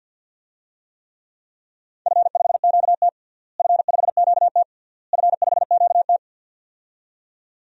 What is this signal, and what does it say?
Morse code, heard as a keyed tone, about 50 wpm, 700 Hz. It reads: F5QT F5QT F5QT
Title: DXCC-100 Call Signs - 4 Characters - Repeated 3x - 50wpm